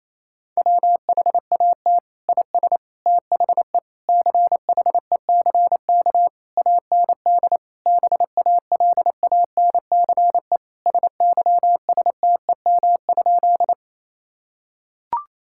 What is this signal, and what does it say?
Morse code, heard as a keyed tone, about 27 wpm, 700 Hz. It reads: WHAT IS THE CHECK AND BALANCE SYSTEM?
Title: Sentences from Top 1000 Words - Review Part 6 - 28wpm